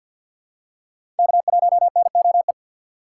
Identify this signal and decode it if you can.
Morse code, heard as a keyed tone, about 52 wpm, 700 Hz. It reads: K1NOE
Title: USA Call Signs - 5 Characters - 4x Character Spacing - 50wpm